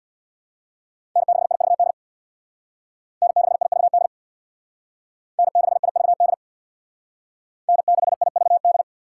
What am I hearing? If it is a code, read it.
Morse code, heard as a keyed tone, about 62 wpm, 700 Hz. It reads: N6IVD N6IVD N6IVD N6IVD
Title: Speed Racing - USA 1x3 Call Signs 50wpm